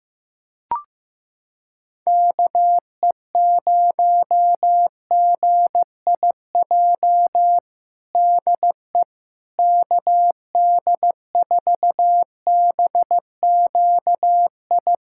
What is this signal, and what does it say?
Morse code, heard as a keyed tone, about 15 wpm, 700 Hz. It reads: KE0GIJ DE KD4BQI
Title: USA Callsign DE Callsign - 6 Characters - 15wpm